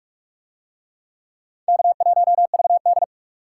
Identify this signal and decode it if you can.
Morse code, heard as a keyed tone, about 42 wpm, 700 Hz. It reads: K1VD